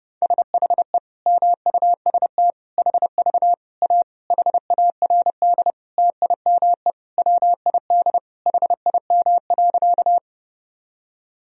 Morse code, heard as a keyed tone, about 32 wpm, 700 Hz. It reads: SHE MUST HV A HARD TIME WID HIM.
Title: Sentences from Top 300 Words and Abbreviations 30wpm